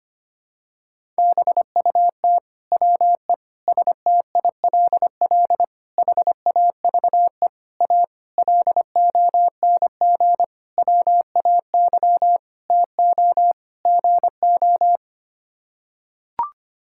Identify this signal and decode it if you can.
Morse code, heard as a keyed tone, about 25 wpm, 700 Hz. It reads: BUT WE STILL HAVE A LONG WAY TO GO
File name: Sets of 9 Words - 25wpm